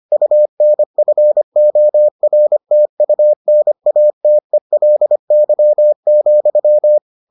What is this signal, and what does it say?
Morse code, heard as a keyed tone, about 25 wpm, 600 Hz. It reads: UNFORTUNATELY,